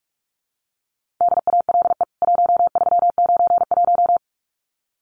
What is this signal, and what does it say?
Morse code, heard as a keyed tone, about 44 wpm, 700 Hz. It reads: DALE 1391
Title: CWT Exchange 45wpm